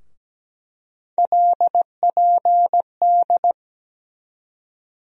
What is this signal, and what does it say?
Morse code, heard as a keyed tone, about 17 wpm, 700 Hz. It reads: LPD
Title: USA Call Sign Postfixes - 3 Letters 17wpm